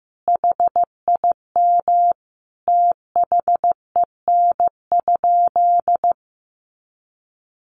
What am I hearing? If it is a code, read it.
Morse code, heard as a keyed tone, about 15 wpm, 700 Hz. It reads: HIM THEN?
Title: Sentences from Top 1400 Words 15wpm